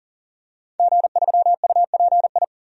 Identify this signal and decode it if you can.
Morse code, heard as a keyed tone, about 37 wpm, 700 Hz. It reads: G3UPI
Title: DXCC-100 Call Signs - 5 Characters - 40wpm